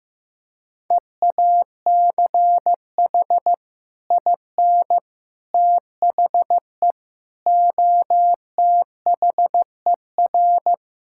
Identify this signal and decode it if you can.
Morse code, heard as a keyed tone, about 15 wpm, 700 Hz. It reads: EACH IN THE OTHER
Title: Sentences from Top 1600 Words 15wpm